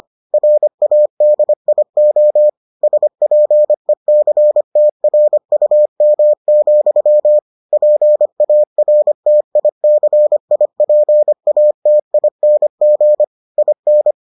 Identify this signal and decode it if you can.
Morse code, heard as a keyed tone, about 25 wpm, 600 Hz. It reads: RADIO SPECTRUM, PARTICIPATING IN